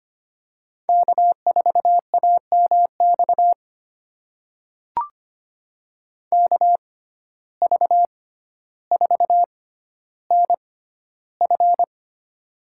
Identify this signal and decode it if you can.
Morse code, heard as a keyed tone, about 25 wpm, spaced as for 13 wpm, 700 Hz. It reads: K4AMX KV4NF
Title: USA Call Signs - 6x Character Spacing - 25wpm